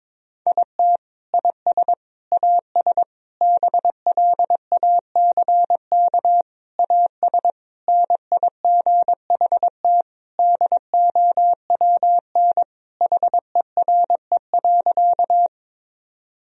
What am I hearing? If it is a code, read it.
Morse code, heard as a keyed tone, about 22 wpm, 700 Hz. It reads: IT IS AS BLACK AS NIGHT DOWN HERE.
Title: Sentences from Top 1000 Words - Review Part 2 - 22wpm